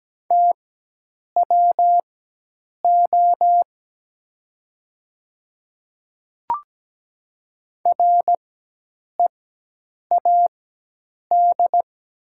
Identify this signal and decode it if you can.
Morse code, heard as a keyed tone, about 17 wpm, spaced as for 8 wpm, 700 Hz. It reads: TWO READ
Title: 4x Character Spacing - Top 200 Words - 17wpm